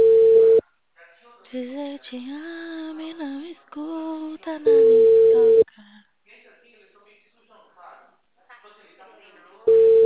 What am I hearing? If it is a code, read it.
Ringback tone (China, 450 Hz), with speech.
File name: out- -2025- - - .28406